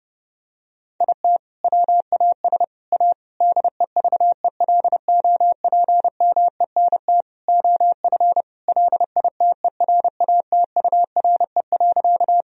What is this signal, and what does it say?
Morse code, heard as a keyed tone, about 30 wpm, 700 Hz. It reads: IT WAS A DEVELOPMENT OF LITERATURE.
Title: Sentences from Top 1000 Words - Review Part 4 - 30wpm